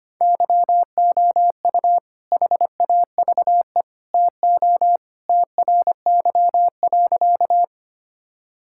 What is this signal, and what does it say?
Morse code, heard as a keyed tone, about 25 wpm, 700 Hz. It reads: YOU HAVE TO TRY.
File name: practice / 25wpm / Sentences with 3-Word Phrases - 25wpm